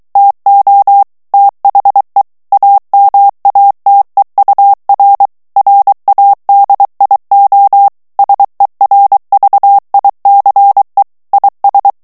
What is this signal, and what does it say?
Morse code, heard as a keyed tone, about 23 wpm, 800 Hz. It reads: TO THE AMATEUR RADIO SERVICE IS
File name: ANGLAIS texte clair 3 vitesse 22 mots minute